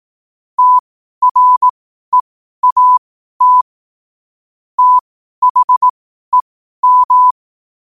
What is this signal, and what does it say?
Morse code, heard as a keyed tone, about 18 wpm, spaced as for 12 wpm, 1000 Hz. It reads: TREAT THEM
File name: qotdfiles / 12wpm